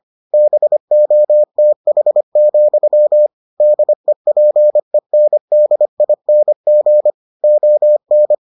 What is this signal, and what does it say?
Morse code, heard as a keyed tone, about 25 wpm, 600 Hz. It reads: BOTH, DEPENDING ON